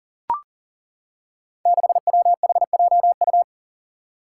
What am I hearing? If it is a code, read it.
Morse code, heard as a keyed tone, about 38 wpm, 700 Hz. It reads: 6WHJU
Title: Five Challenging Letter-Number Combinations 40wpm